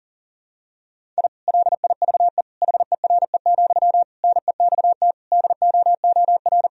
Morse code, heard as a keyed tone, about 40 wpm, 700 Hz. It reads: I LIVE HERE, NEXT DOOR
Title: Sentences from Top 1000 Words - Review Part 1 - 40wpm